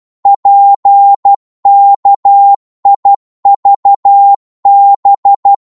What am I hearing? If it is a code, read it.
Morse code, heard as a keyed tone, about 12 wpm, 800 Hz. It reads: PKIVB